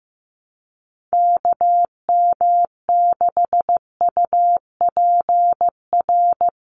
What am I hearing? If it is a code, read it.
Morse code, heard as a keyed tone, about 15 wpm, 700 Hz. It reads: KM6UPR
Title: USA Call Signs 15wpm